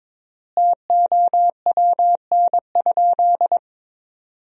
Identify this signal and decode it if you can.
Morse code, heard as a keyed tone, about 22 wpm, 700 Hz. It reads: TOWN?
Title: 3x Word-Spacing - Sentences from Top 500 Words - 22wpm